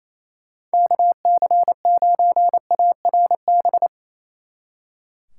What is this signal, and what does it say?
Morse code, heard as a keyed tone, about 34 wpm, 700 Hz. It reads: KC9ARB
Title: Speed Racing - USA Call Signs 28wpm